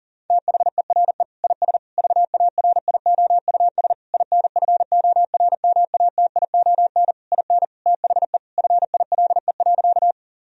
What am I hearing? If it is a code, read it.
Morse code, heard as a keyed tone, about 37 wpm, 700 Hz. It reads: THERE IS VARIOUS INFORMATION IN THE FILE.
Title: Sentences from Top 1000 Words - Review Part 4 - 40wpm